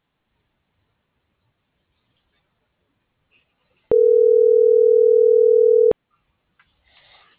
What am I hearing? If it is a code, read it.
Ringback tone (North America), 440 + 480 Hz.